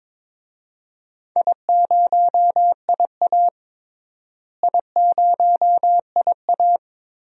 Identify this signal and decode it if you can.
Morse code, heard as a keyed tone, about 22 wpm, 700 Hz. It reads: I0IA I0IA
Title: DXCC-100 Call Signs - 4 Characters - Repeated 2x - 22wpm